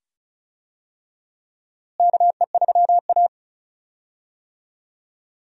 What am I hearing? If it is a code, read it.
Morse code, heard as a keyed tone, about 36 wpm, 700 Hz. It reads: KE3A
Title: USA Call Signs - 4 Characters - Repeated 2x - 35wpm